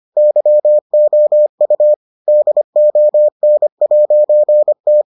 Morse code, heard as a keyed tone, about 25 wpm, 600 Hz. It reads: YOU DON'T